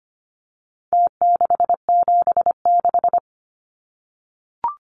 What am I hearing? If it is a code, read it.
Morse code, heard as a keyed tone, about 25 wpm, 700 Hz. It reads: T676
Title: Numbers - Contest Style - 4-Digit 25wpm